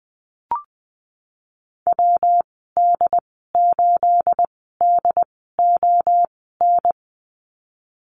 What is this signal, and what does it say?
Morse code, heard as a keyed tone, about 20 wpm, spaced as for 15 wpm, 700 Hz. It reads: WD8DON
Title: 238 Lesson 38g - Letter q - Call Signs 20wpm